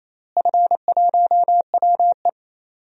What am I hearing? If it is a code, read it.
Morse code, heard as a keyed tone, about 28 wpm, 700 Hz. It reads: F1WE
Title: DXCC-100 Call Signs - 4 Characters - Repeated 2x - 28wpm